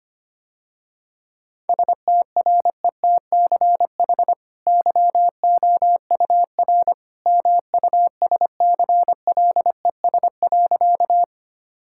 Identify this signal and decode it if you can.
Morse code, heard as a keyed tone, about 25 wpm, 700 Hz. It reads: STRETCH YOUR MUSCLES.
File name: Easy Sentences with Hints - 25wpm